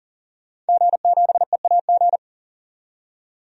Morse code, heard as a keyed tone, about 40 wpm, 700 Hz. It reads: G7EAG
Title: DXCC-100 Call Signs - 5 Characters - 40wpm